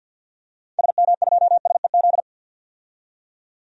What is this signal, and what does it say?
Morse code, heard as a keyed tone, about 52 wpm, 700 Hz. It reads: SM2SEZ
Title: DXCC-100 Call Signs - 6 Characters - 50wpm